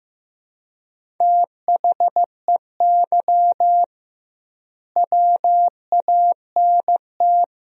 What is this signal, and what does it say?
Morse code, heard as a keyed tone, about 15 wpm, 700 Hz. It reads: THEY WANT
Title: 2x Word-Spacing - Sentences from Top 200 Words - 15wpm